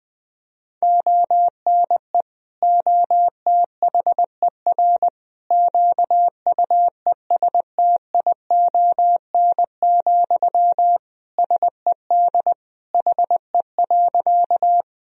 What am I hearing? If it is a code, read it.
Morse code, heard as a keyed tone, about 21 wpm, 700 Hz. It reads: ONE OTHER QUESTION, SED HE.
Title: Sentences from Top 400 Words and Abbreviations 20wpm